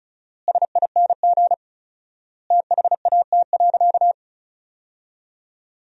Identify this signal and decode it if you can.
Morse code, heard as a keyed tone, about 36 wpm, 700 Hz. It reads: SING THAT.
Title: 4x Word-Spacing - Sentences from Top 500 Words - 35wpm